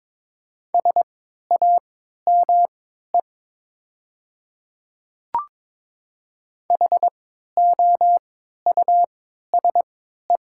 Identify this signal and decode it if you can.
Morse code, heard as a keyed tone, about 22 wpm, spaced as for 12 wpm, 700 Hz. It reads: SAME HOUSE
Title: 3x Character Spacing - Top 200 Words - 22wpm